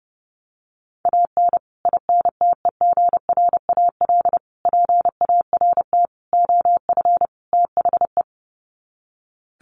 Morse code, heard as a keyed tone, about 30 wpm, 700 Hz. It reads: AN INTEGRAL PART OF THE
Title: Sets of 5 Words 30wpm